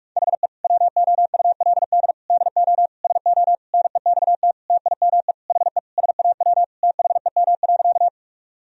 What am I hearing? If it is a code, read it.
Morse code, heard as a keyed tone, about 41 wpm, 700 Hz. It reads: HE WOULD DO SO NEXT TIME HE SAW THEM.
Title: Sentences from Top 1000 Words - Review Part 1 - 45wpm